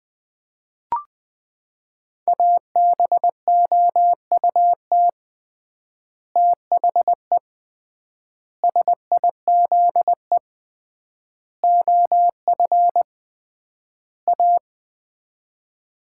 Morse code, heard as a keyed tone, about 19 wpm, 700 Hz. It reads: ABOUT THE SIZE OF A
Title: 3x Word-Spacing - Sets of 5 Words - 20wpm